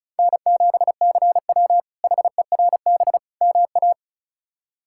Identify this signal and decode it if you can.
Morse code, heard as a keyed tone, about 36 wpm, 700 Hz. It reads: N7CW HERB MA
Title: K1USN SST QSO Practice - 35wpm